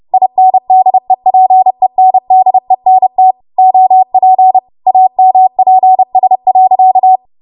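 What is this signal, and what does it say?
Morse code, heard as a keyed tone, about 30 wpm, 750 Hz. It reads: INDEPENDENT OP AMPS.
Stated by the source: W1AW Web Code Practice Files, 30 WPM run from ARRL; text from April 2023 QST.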